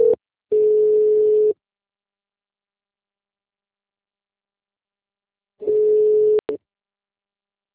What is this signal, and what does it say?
Ringback tone (Europe, 425 Hz).